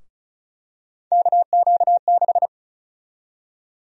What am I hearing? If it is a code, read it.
Morse code, heard as a keyed tone, about 36 wpm, 700 Hz. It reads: KQ6
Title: USA Call Sign Prefixes - Repeated 2x - 35wpm